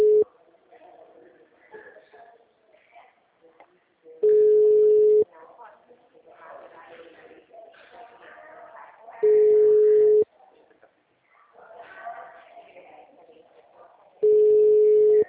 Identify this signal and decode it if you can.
Ringback tone (Europe, 425 Hz), with speech.